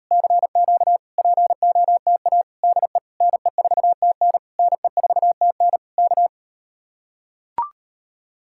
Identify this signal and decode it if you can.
Morse code, heard as a keyed tone, about 38 wpm, 700 Hz. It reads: CQ POTA DE NE4TN NE4TN K